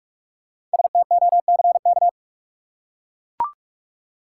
Morse code, heard as a keyed tone, about 43 wpm, 700 Hz. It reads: STOCK
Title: Five Letter Words - 45wpm